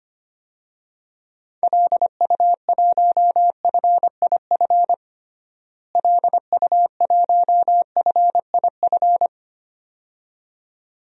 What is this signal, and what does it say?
Morse code, heard as a keyed tone, about 25 wpm, 700 Hz. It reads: LU1FIF LU1FIF
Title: DXCC-100 Call Signs - 6 Characters - Repeated 2x - 25wpm